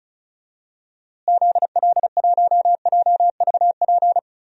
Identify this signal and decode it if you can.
Morse code, heard as a keyed tone, about 34 wpm, 700 Hz. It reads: ZL1JVP
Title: DXCC-100 Call Signs - 6 Characters - Repeated 3x - 35wpm